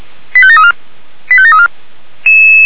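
Five-tone selective call (ZVEI), heard as quick runs of five tones, tones of about 70 ms. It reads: 85734 85734
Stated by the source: ZVEI five-tone selective call received on a software-defined radio.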